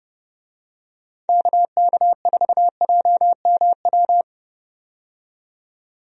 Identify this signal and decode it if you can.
Morse code, heard as a keyed tone, about 30 wpm, 700 Hz. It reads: KK4JMW
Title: USA Call Signs - 6 Characters - 30wpm